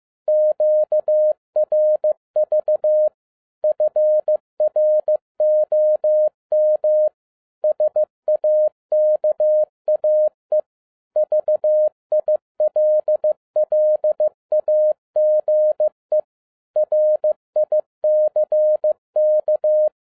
Morse code, heard as a keyed tone, about 15 wpm, 610 Hz. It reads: QRV FROM SAKAE VILLAGE RICK
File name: 2011-10-mp3 / n02-FIELD DAY CONTEST QRV FROM SAKAE VILLAGE 15wpm